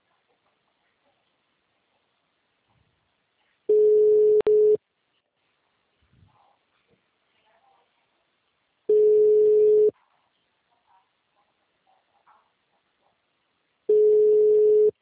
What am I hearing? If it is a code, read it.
Ringback tone (Europe, 425 Hz).
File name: out- -2010- - - .92868